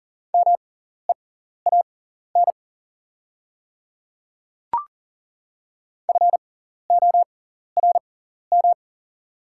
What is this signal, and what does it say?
Morse code, heard as a keyed tone, about 40 wpm, spaced as for 14 wpm, 690 Hz. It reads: MEAN FORM
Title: 6x Character Spacing - Top 300 Words - 40wpm